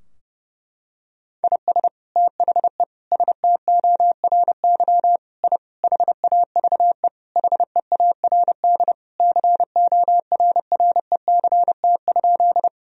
Morse code, heard as a keyed tone, about 30 wpm, 700 Hz. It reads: IS THE STORY I HAVE HEARD CORRECT?
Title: Sentences from Top 1000 Words - Review Part 2 - 30wpm